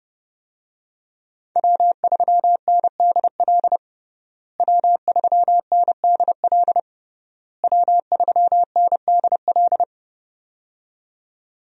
Morse code, heard as a keyed tone, about 30 wpm, 700 Hz. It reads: W3NDL W3NDL W3NDL
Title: USA Call Signs - 5 Characters - Repeated 3x - 30wpm